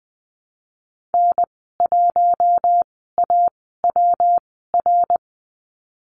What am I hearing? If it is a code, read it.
Morse code, heard as a keyed tone, about 19 wpm, spaced as for 14 wpm, 700 Hz. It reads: N1AWR
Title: 154 Lesson 25g - Letter p - Call Signs 20wpm